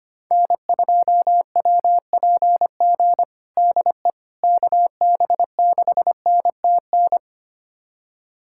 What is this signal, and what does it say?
Morse code, heard as a keyed tone, about 25 wpm, 700 Hz. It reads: N2WPG DE KB6NTN
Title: USA Callsign DE Callsign 25wpm